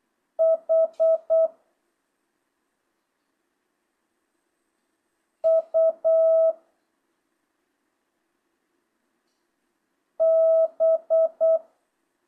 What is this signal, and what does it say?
Morse code, heard as a keyed tone, about 8 wpm, 650 Hz. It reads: H U B
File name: jh8cbh / 2025cw